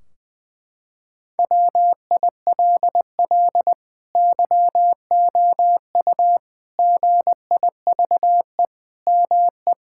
Morse code, heard as a keyed tone, about 21 wpm, 700 Hz. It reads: WILL YOU GIVE ME
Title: Sentences from Top 1600 Words 20wpm